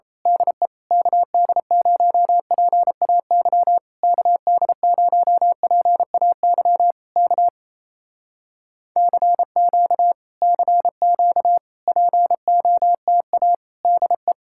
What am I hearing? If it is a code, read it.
Morse code, heard as a keyed tone, about 27 wpm, 700 Hz. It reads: DE KD0PAY KD0PAY K CQ CQ POTA DE
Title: Speed-Racing Common QSO Phrases Encore - 28wpm